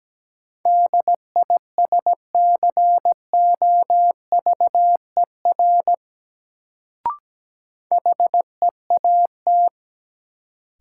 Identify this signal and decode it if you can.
Morse code, heard as a keyed tone, about 17 wpm, 700 Hz. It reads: DISCOVER HEAT
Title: Sentences from Top 1100 Words - New Word Drill Alt - 17wpm